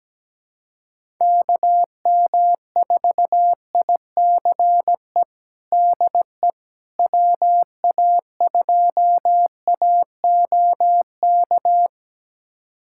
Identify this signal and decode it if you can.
Morse code, heard as a keyed tone, about 17 wpm, 700 Hz. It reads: KM4ICE DE WA2AOK
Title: USA Callsign DE Callsign - 6 Characters - 17wpm